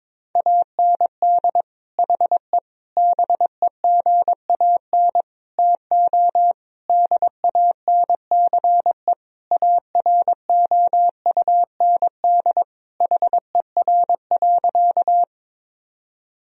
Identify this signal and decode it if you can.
Morse code, heard as a keyed tone, about 21 wpm, 700 Hz. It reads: AND HE BEGAN TO DANCE AROUND HER.
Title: Sentences from Top 1000 Words - Review Part 4 - 22wpm